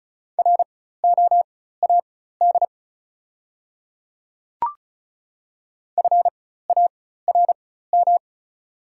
Morse code, heard as a keyed tone, about 36 wpm, spaced as for 16 wpm, 700 Hz. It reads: ROAD FARM
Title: 4x Character Spacing - Top 500 Words - 35wpm